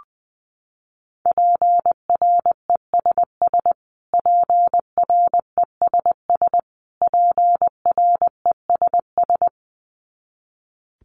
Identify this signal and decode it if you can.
Morse code, heard as a keyed tone, about 20 wpm, 700 Hz. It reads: PRESS PRESS PRESS
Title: Top 300 Words Repeated 20wpm